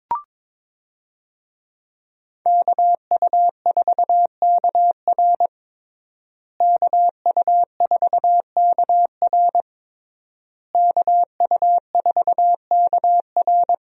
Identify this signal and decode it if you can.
Morse code, heard as a keyed tone, about 21 wpm, 700 Hz. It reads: KU4KR KU4KR KU4KR
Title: USA Call Signs - 5 Characters - Repeated 3x - 22wpm